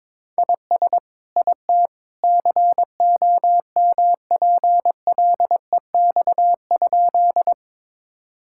Morse code, heard as a keyed tone, about 22 wpm, 700 Hz. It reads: IS IT COMPLEX?
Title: Sentences from Top 1000 Words - Review Part 6 - 22wpm